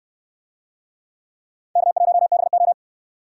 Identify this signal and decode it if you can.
Morse code, heard as a keyed tone, about 66 wpm, 700 Hz. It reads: K1BO